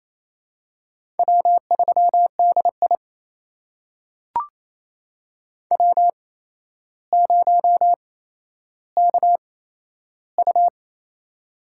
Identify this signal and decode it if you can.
Morse code, heard as a keyed tone, about 28 wpm, spaced as for 12 wpm, 700 Hz. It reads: W3DI W0KU
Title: USA Call Signs - 4 Characters - 8x Character Spacing - 28wpm